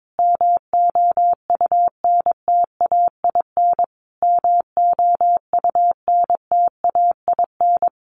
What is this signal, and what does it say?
Morse code, heard as a keyed tone, about 22 wpm, 700 Hz. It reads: MOUNTAIN MOUNTAIN
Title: Top 300 Words Repeated 22wpm